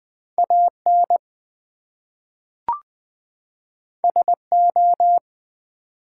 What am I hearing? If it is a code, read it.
Morse code, heard as a keyed tone, about 20 wpm, 700 Hz. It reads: AN SO
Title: Two Letter Words - 20wpm